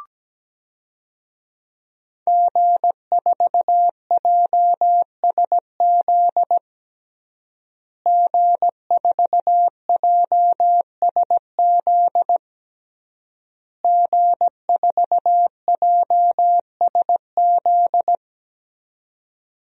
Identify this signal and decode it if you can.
Morse code, heard as a keyed tone, about 17 wpm, 700 Hz. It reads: G4JSZ G4JSZ G4JSZ
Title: DXCC-100 Call Signs - 5 Characters - Repeated 3x - 17wpm